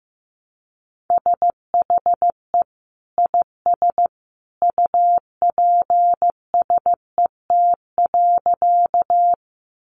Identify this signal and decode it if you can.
Morse code, heard as a keyed tone, about 15 wpm, 700 Hz. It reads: SHE IS UPSET.
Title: Sentences from Top 800 Words 15wpm